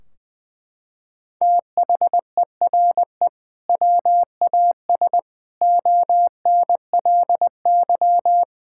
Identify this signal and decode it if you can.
Morse code, heard as a keyed tone, about 19 wpm, 700 Hz. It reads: THERE WAS ONLY
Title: Sentences from Top 1600 Words 20wpm